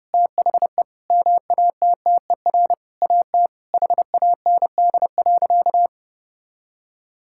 Morse code, heard as a keyed tone, about 30 wpm, 700 Hz. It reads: THE MATTER AT HAND.
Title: Sentences from Top 1600 Words 30wpm